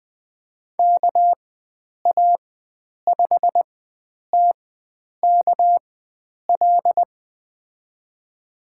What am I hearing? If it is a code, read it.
Morse code, heard as a keyed tone, about 20 wpm, spaced as for 9 wpm, 700 Hz. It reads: KA5TKL
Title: USA Call Signs - 6 Characters - 4x Character Spacing - 20wpm